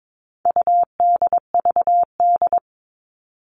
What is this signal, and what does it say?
Morse code, heard as a keyed tone, about 22 wpm, 700 Hz. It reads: UDVD
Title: All Cut Numbers - 4-Digit 22wpm